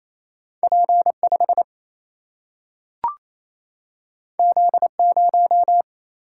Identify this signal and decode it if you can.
Morse code, heard as a keyed tone, about 28 wpm, 700 Hz. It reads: P 5 Z 0
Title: Single-Letter Single-Number 28wpm